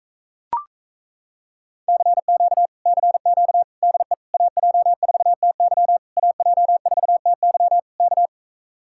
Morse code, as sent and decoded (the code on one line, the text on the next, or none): -.-. --.- / -.-. --.- / -.. / .- .--- ....- - -.-- / .- .--- ....- - -.-- / -.-
CQ CQ DE AJ4TY AJ4TY K